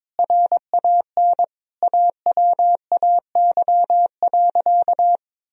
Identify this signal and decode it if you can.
Morse code, heard as a keyed tone, about 21 wpm, 700 Hz. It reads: RAN AWAY.
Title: Sentences from Top 1000 Words - Review Part 3 - 22wpm